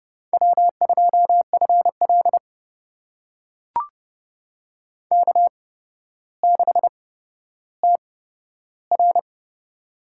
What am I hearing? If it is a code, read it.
Morse code, heard as a keyed tone, about 30 wpm, spaced as for 13 wpm, 700 Hz. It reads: W2FL K6TR
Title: USA Call Signs - 4 Characters - 8x Character Spacing - 30wpm